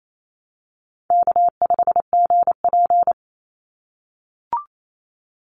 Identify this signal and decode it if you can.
Morse code, heard as a keyed tone, about 28 wpm, 700 Hz. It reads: K5GP
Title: Easy Call Signs - 28wpm